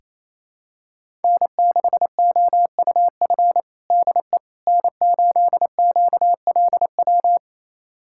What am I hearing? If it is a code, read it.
Morse code, heard as a keyed tone, about 28 wpm, 700 Hz. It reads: N6OUF DE N8QLW
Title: USA Callsign DE Callsign - 5 Characters - 28wpm